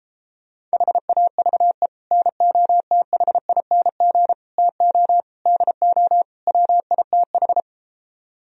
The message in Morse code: HAVE NOTHING TO DO WITH

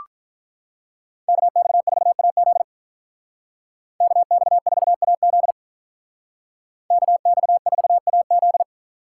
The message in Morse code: KX4AZ KX4AZ KX4AZ

